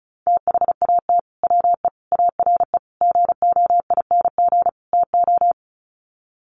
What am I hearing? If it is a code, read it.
Morse code, heard as a keyed tone, about 34 wpm, 700 Hz. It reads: THAT WE ARE GOING TO
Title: Sets of 5 Words Encore 35wpm